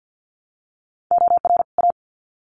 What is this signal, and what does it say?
Morse code, heard as a keyed tone, about 50 wpm, 700 Hz. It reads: OR A
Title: Sets of 2 Words 50wpm